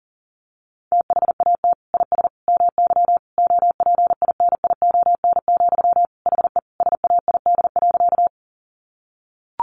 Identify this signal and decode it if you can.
Morse code, heard as a keyed tone, about 40 wpm, 700 Hz. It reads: THAT IS MY OPINION, HE SAID.
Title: Sentences from Top 1200 Words 40wpm